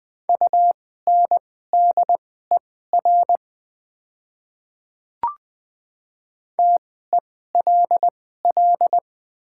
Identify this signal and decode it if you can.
Morse code, heard as a keyed tone, about 20 wpm, spaced as for 15 wpm, 700 Hz. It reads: UNDER TELL
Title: 2x Character Spacing - Top 200 Words - 20wpm